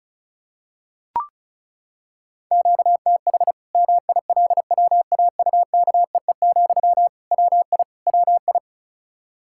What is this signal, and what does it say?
Morse code, heard as a keyed tone, about 34 wpm, 700 Hz. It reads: QTH MILWAUKEE, WI WI
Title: K1USN SST QSO Practice 35wpm